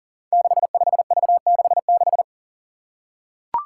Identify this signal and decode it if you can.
Morse code, heard as a keyed tone, about 38 wpm, 700 Hz. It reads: 65V66
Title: Five Challenging Letter-Number Combinations 40wpm